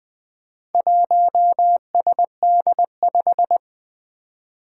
Morse code, heard as a keyed tone, about 20 wpm, 700 Hz. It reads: 1SD5
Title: Four Challenging Letter-Number Combinations 20wpm